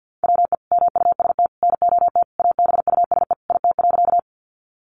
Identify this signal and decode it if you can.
Morse code, heard as a keyed tone, about 50 wpm, 700 Hz. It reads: WE MUST NOT ABUSE IT.
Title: Sentences from Top 1300 Words 50wpm